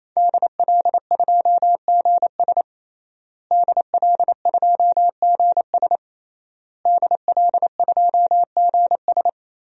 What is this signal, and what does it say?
Morse code, heard as a keyed tone, about 28 wpm, 700 Hz. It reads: DL2GS DL2GS DL2GS
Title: DXCC-100 Call Signs - 5 Characters - Repeated 3x - 28wpm